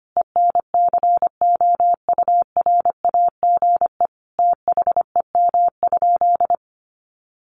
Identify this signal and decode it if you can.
Morse code, heard as a keyed tone, about 25 wpm, 700 Hz. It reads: ENCOURAGE THEM?
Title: Sentences from Top 1400 Words 25wpm